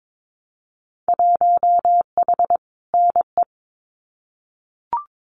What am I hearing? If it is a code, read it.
Morse code, heard as a keyed tone, about 22 wpm, 700 Hz. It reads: 1H NE
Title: Winter Field Day Exchanges Encore 22wpm